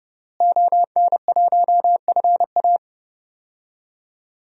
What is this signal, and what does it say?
Morse code, heard as a keyed tone, about 30 wpm, 700 Hz. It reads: ON1FA
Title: DXCC-100 Call Signs - 5 Characters - Repeated 2x - 30wpm